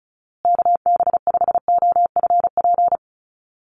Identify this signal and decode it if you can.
Morse code, heard as a keyed tone, about 34 wpm, 700 Hz. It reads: KB5OFP
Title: USA Call Signs Repeated 2x 35wpm